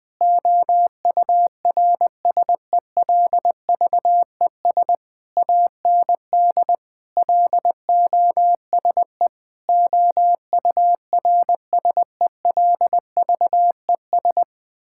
Morse code, heard as a keyed tone, about 21 wpm, 700 Hz. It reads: OURSELVES AND LOSE OURSELVES
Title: Difficult Sentences with Hints - 20wpm